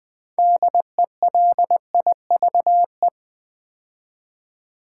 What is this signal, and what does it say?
Morse code, heard as a keyed tone, about 20 wpm, 700 Hz. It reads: DELIVE
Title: Build A Word - 3 Letters At A Time - 8-Letter Words - 20wpm